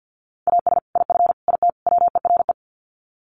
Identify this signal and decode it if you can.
Morse code, heard as a keyed tone, about 50 wpm, 700 Hz. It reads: AS IF IT WERE
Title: Sets of 4 Words 50wpm